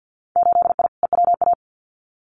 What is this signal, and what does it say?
Morse code, heard as a keyed tone, about 50 wpm, 700 Hz. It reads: 8I EWA
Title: Winter Field Day Exchanges Encore 50wpm